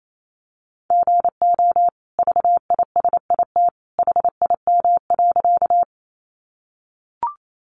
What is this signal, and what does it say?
Morse code, heard as a keyed tone, about 28 wpm, 700 Hz. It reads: GO VISIT HIM.
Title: Sentences from Top 900 Words 28wpm